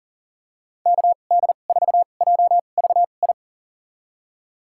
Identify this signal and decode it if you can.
Morse code, heard as a keyed tone, about 38 wpm, spaced as for 28 wpm, 700 Hz. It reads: KD4JVI
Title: USA Call Signs - 6 Characters - 2x Character Spacing - 40wpm